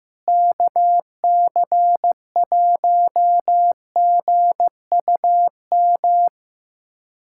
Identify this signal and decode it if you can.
Morse code, heard as a keyed tone, about 16 wpm, 700 Hz. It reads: KC1GUM
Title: USA Callsign DE Callsign - 6 Characters - 15wpm